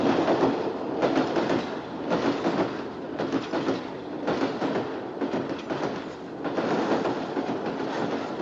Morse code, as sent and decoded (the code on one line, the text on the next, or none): none